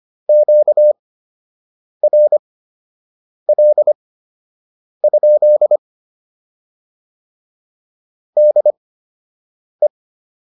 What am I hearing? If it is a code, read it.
Morse code, heard as a keyed tone, about 25 wpm, spaced as for 7 wpm, 600 Hz. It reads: QRL? DE